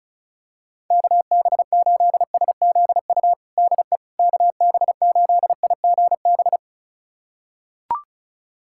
Morse code, heard as a keyed tone, about 34 wpm, 700 Hz. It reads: KB8SZU DE KB8IGB
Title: USA Callsign DE Callsign - 6 Characters - 35wpm